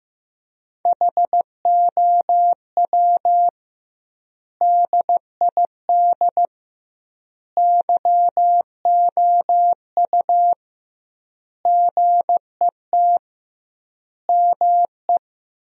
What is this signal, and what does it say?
Morse code, heard as a keyed tone, about 15 wpm, 700 Hz. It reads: HOW DID YOU GET ME
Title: 2x Word-Spacing - Sentences from Top 200 Words - 15wpm